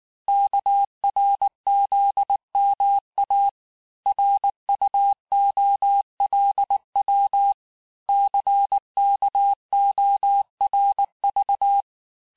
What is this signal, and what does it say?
Morse code, heard as a keyed tone, about 19 wpm, 800 Hz. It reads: KRZMA RUOLW CKORV
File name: cours-du-lundi files / cours-1595-teste-CHIFFRE-vitesse18-MPM